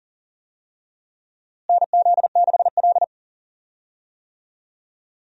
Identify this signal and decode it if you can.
Morse code, heard as a keyed tone, about 39 wpm, 700 Hz. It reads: NZ6L
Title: USA Call Signs - 4 Characters - Repeated 2x - 40wpm